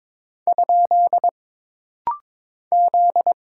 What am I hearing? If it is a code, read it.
Morse code, heard as a keyed tone, about 22 wpm, 700 Hz. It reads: ? Z
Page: This rapid-fire timing is geared to push your ICR (Instant Character Recognition) skills.